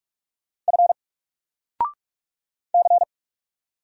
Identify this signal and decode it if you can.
Morse code, heard as a keyed tone, about 44 wpm, 700 Hz. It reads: F C
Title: Single Letter-Number 45wpm